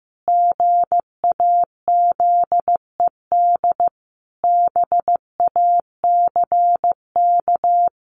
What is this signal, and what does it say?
Morse code, heard as a keyed tone, about 14 wpm, 700 Hz. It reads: GAZED BACK
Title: A Sound Of Thunder 15wpm